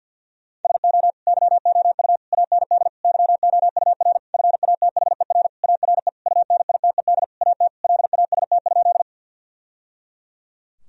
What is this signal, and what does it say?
Morse code, heard as a keyed tone, about 50 wpm, 700 Hz. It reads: SO YOU AND YOUR FATHER ARE UNITED AT LAST?
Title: Sentences from Top 1000 Words - Review Part 4 - 50wpm